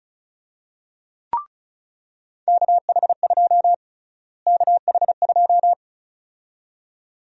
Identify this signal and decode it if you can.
Morse code, heard as a keyed tone, about 36 wpm, 700 Hz. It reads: KH2 KH2
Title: USA Call Sign Prefixes - Repeated 2x - 35wpm